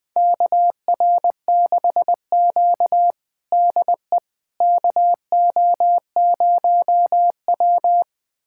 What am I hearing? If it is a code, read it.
Morse code, heard as a keyed tone, about 20 wpm, 700 Hz. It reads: KR6Q DE KO0W
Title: USA Callsign DE Callsign - 4 Characters - 20wpm